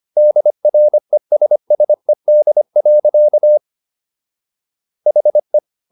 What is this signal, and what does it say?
Morse code, heard as a keyed tone, about 25 wpm, 600 Hz. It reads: DRESSED. HE